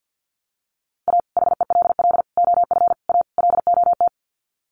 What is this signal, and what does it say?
Morse code, heard as a keyed tone, about 50 wpm, 700 Hz. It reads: A HELL OF A LOT
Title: Sets of 5 Words Encore 50wpm